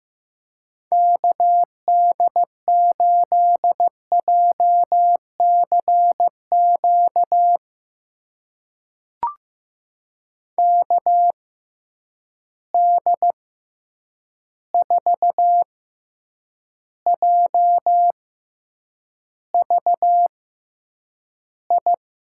Morse code, heard as a keyed tone, about 15 wpm, spaced as for 8 wpm, 700 Hz. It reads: KD8JCQ KD4JVI
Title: USA Call Signs - 6 Characters - 6x Character Spacing - 15wpm